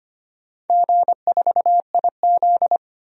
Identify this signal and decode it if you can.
Morse code, heard as a keyed tone, about 25 wpm, 700 Hz. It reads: G4IZ
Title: DXCC-100 Call Signs - 4 Characters - 25wpm